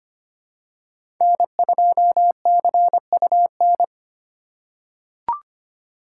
Morse code, heard as a keyed tone, about 25 wpm, 700 Hz. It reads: N2CUN
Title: USA Call Signs - 5 Characters - 4x Character Spacing - 25wpm